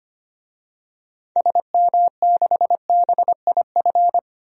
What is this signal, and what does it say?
Morse code, heard as a keyed tone, about 25 wpm, 700 Hz. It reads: SM6BIF
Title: DXCC-100 Call Signs - 6 Characters - 25wpm